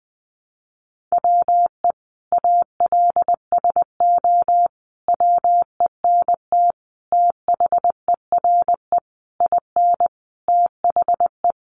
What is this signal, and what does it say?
Morse code, heard as a keyed tone, about 20 wpm, 700 Hz. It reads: WE ALSO WENT THERE IN THE
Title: Sentences from Top 1200 Words 20wpm